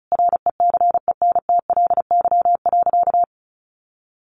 Morse code, heard as a keyed tone, about 34 wpm, 700 Hz. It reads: RECENTLY.
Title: Sentences from Top 1200 Words 35wpm